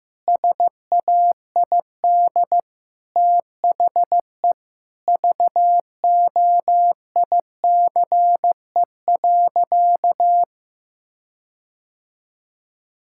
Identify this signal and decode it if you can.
Morse code, heard as a keyed tone, about 15 wpm, 700 Hz. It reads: SAID THE VOICE.
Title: Sentences from Top 1600 Words 15wpm